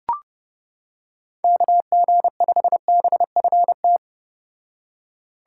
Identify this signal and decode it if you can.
Morse code, heard as a keyed tone, about 31 wpm, 700 Hz. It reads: KG5BFT